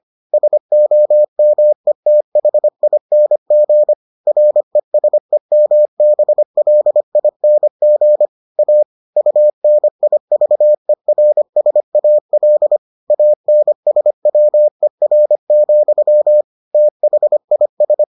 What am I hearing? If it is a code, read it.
Morse code, heard as a keyed tone, about 25 wpm, 600 Hz. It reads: SOMETHING RESEMBLING A UNIVERSAL ANSWER, THIS